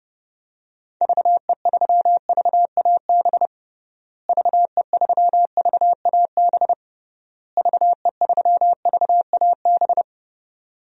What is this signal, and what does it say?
Morse code, heard as a keyed tone, about 32 wpm, 700 Hz. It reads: VE3VAB VE3VAB VE3VAB
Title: DXCC-100 Call Signs - 6 Characters - Repeated 3x - 30wpm